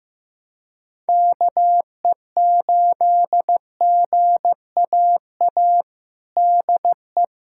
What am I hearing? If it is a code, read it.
Morse code, heard as a keyed tone, about 15 wpm, 700 Hz. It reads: KE8GAA DE
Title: USA Callsign DE Callsign 15wpm